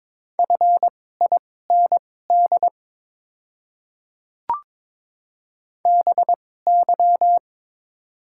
Morse code, heard as a keyed tone, about 22 wpm, spaced as for 16 wpm, 700 Hz. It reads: FIND BY
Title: 2x Character Spacing - Top 100 Words - 22wpm